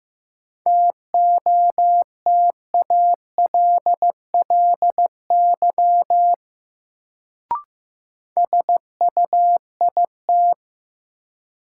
Morse code, heard as a keyed tone, about 15 wpm, 700 Hz. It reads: TOTALLY SUIT
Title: Sentences from Top 1600 Words - New Word Drill - 15wpm